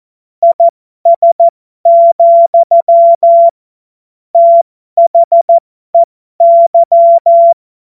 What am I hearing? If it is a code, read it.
Morse code, heard as a keyed tone, about 14 wpm, spaced as for 12 wpm, 680 Hz. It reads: IS, THEY
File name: Morsekurs / Witze